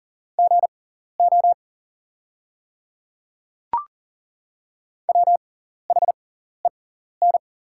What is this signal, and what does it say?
Morse code, heard as a keyed tone, about 40 wpm, spaced as for 14 wpm, 700 Hz. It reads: GO WHEN
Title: 6x Character Spacing - Top 100 Words - 40wpm